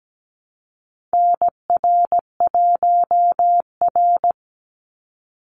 Morse code, heard as a keyed tone, about 17 wpm, 700 Hz. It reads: NR1R
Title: Call Signs Repeated 2x - 17wpm